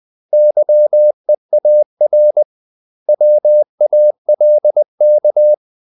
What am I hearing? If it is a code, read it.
Morse code, heard as a keyed tone, about 20 wpm, 600 Hz. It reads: YEAR WALK
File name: CW / Practice / 500MostCommonEnglishWords20